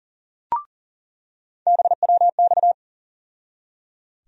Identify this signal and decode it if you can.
Morse code, heard as a keyed tone, about 39 wpm, 700 Hz. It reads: BWX
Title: USA Call Sign Postfixes - 3 Letters 40wpm